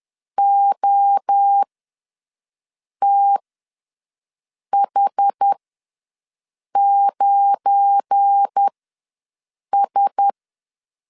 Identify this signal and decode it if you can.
Morse code, heard as a keyed tone, about 10 wpm, spaced as for 5 wpm, 800 Hz. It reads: OTH9S
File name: LEZIONI / 7 GF9Z / 7 GF9 ws 10wpm 4034